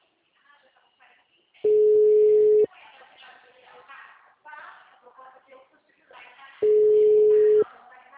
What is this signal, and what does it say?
Ringback tone (Europe, 425 Hz), with speech.
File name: out- -2016- - - .3960